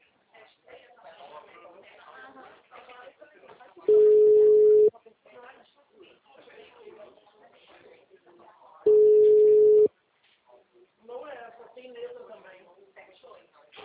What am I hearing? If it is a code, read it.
Ringback tone (Europe, 425 Hz), with speech.